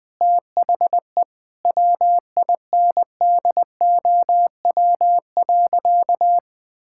Morse code, heard as a keyed tone, about 21 wpm, 700 Hz. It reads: THE WINDOW.
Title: Sentences with 2-Word Phrases - 20wpm